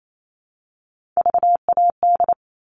Morse code, heard as a keyed tone, about 28 wpm, 700 Hz. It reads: VAD
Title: All Cut Numbers - 3-Digit 28wpm